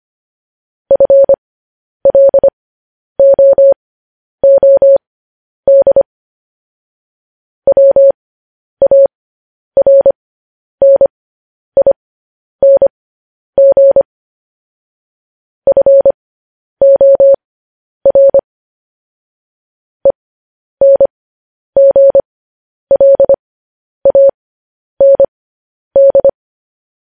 Morse code, heard as a keyed tone, about 25 wpm, spaced as for 10 wpm, 550 Hz. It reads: FLOOD WARNING FOR ENGLAND
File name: podcast / News headlines-1x-25x10